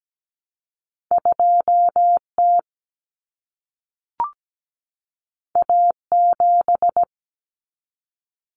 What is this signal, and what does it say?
Morse code, heard as a keyed tone, about 17 wpm, 700 Hz. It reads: 2T A7
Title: Common Cut Numbers - 2-Digit 17wpm